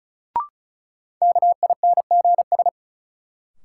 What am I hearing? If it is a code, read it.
Morse code, heard as a keyed tone, about 36 wpm, 700 Hz. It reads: KINGS
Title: Sentences from Top 1000 Words - Review Part 6 - New Word Drill - 35wpm